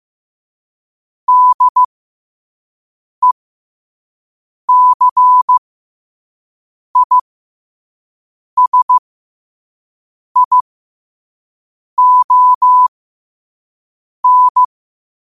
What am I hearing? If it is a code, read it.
Morse code, heard as a keyed tone, about 15 wpm, spaced as for 5 wpm, 1000 Hz. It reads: DECISION
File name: qotdfiles / 5wpm